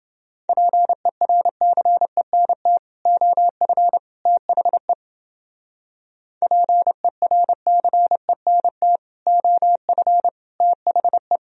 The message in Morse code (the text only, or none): PERCENT OF THE PERCENT OF THE